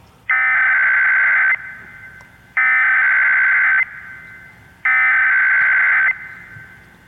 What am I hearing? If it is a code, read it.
Emergency Alert System (SAME) tones (header bursts), heard as harsh buzzing bursts. It is heard from 0.3 to 6.1 s.